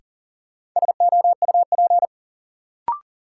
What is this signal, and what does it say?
Morse code, heard as a keyed tone, about 40 wpm, 700 Hz. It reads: SOUP
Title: Sentences from Top 1600 Words - New Word Drill Alt - 40wpm